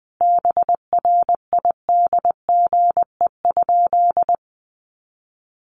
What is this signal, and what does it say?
Morse code, heard as a keyed tone, about 20 wpm, 700 Hz. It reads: BRIDGE?
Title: Sentences from Top 1200 Words 20wpm